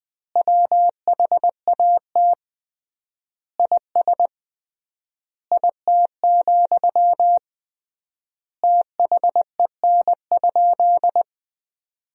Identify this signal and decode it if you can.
Morse code, heard as a keyed tone, about 21 wpm, 700 Hz. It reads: WHAT IS IT, THEN?
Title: 3x Word-Spacing - Sentences from Top 200 Words - 20wpm